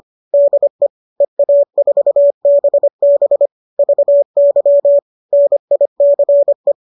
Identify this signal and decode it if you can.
Morse code, heard as a keyed tone, about 25 wpm, 570 Hz. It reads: DE EA4BB VY NICE